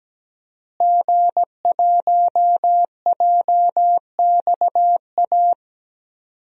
Morse code, heard as a keyed tone, about 17 wpm, 700 Hz. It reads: G1JXA
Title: DXCC-100 Call Signs - 5 Characters - Repeated 2x - 17wpm